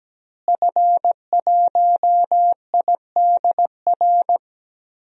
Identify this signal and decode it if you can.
Morse code, heard as a keyed tone, about 17 wpm, 700 Hz. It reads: F1IDR
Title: DXCC-100 Call Signs - 5 Characters - Repeated 3x - 17wpm